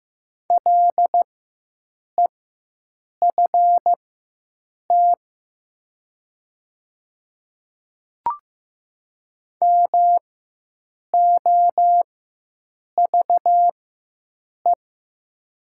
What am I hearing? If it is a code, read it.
Morse code, heard as a keyed tone, about 15 wpm, spaced as for 7 wpm, 700 Hz. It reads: LEFT MOVE
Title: 4x Character Spacing - Top 500 Words - 15wpm